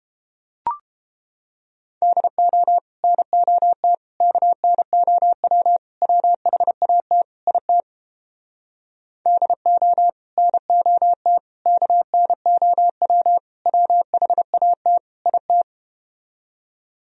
Morse code, heard as a keyed tone, about 33 wpm, 700 Hz. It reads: DO NOT KNOW WHAT IT DO NOT KNOW WHAT IT